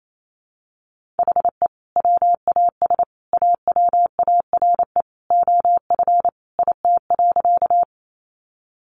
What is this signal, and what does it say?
Morse code, heard as a keyed tone, about 28 wpm, 700 Hz. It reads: HE WAS AWARE OF IT.
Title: Sentences from Top 1200 Words 28wpm